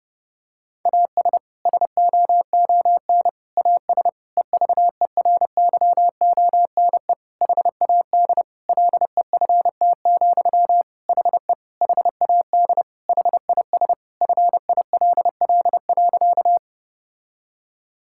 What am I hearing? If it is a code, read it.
Morse code, heard as a keyed tone, about 32 wpm, 700 Hz. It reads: AS SOON AS EVERYONE HAD LEFT, HE HAD HIS FILL.
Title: Sentences from Top 1000 Words - Review Part 5 - 30wpm